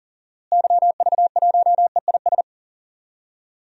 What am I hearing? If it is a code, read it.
Morse code, heard as a keyed tone, about 40 wpm, 700 Hz. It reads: YV1EIS
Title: DXCC-100 Call Signs - 6 Characters - Repeated 3x - 40wpm